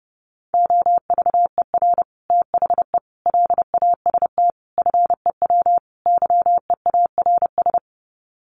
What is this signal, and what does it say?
Morse code, heard as a keyed tone, about 30 wpm, 700 Hz. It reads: OVER THE LAST FEW YEARS
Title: Sets of 5 Words Encore 30wpm